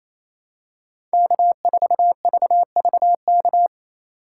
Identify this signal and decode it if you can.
Morse code, heard as a keyed tone, about 28 wpm, 700 Hz. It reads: K4VVK
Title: USA Call Signs - 5 Characters - 8x Character Spacing - 28wpm